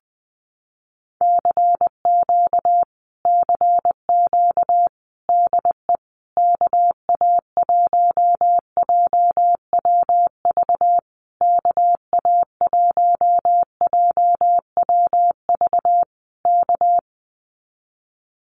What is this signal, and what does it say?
Morse code, heard as a keyed tone, about 20 wpm, 700 Hz. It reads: CQ CQ DE KA1JWV KA1JWV K
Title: Common QSO Phrases Encore 20wpm